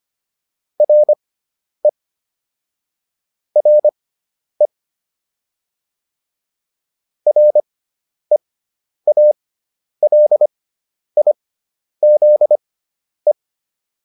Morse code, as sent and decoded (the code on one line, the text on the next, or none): .-. / .-. / .-. .- .-.. .. --.. .
RE RE REALIZE